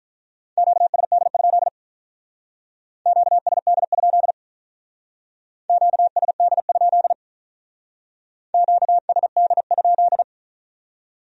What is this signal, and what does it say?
Morse code, heard as a keyed tone, about 45 wpm, 700 Hz. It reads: QSD? QSD? QSD? QSD?